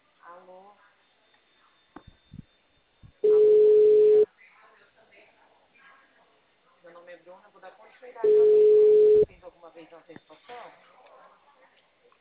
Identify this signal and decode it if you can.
Ringback tone (Europe, 425 Hz), with speech.